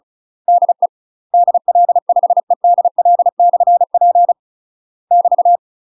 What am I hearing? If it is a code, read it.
Morse code, heard as a keyed tone, about 34 wpm, 700 Hz. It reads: DE DL5EDL/P <BT>
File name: UK MASTER QSOs TRAINING / 35WPM / UK MASTER QSOs 0034